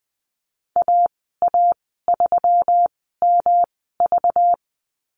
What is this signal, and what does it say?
Morse code, heard as a keyed tone, about 20 wpm, spaced as for 14 wpm, 700 Hz. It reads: AA3MV
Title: 200 Lesson 32g - Letter k - Call Signs 20wpm